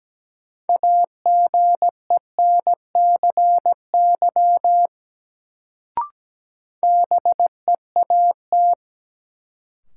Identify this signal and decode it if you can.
Morse code, heard as a keyed tone, about 17 wpm, 700 Hz. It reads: AGENCY BEAT
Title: Sentences from Top 1300 Words - New Word Drill Alt - 17wpm